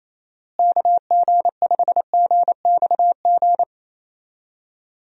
Morse code, heard as a keyed tone, about 28 wpm, 700 Hz. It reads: KG5GXG
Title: USA Call Signs - 6 Characters - Repeated 2x - 28wpm